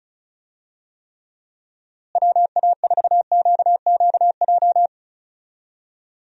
Morse code, heard as a keyed tone, about 34 wpm, 700 Hz. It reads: WA4QQJ
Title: USA Call Signs - 6 Characters - Repeated 2x - 35wpm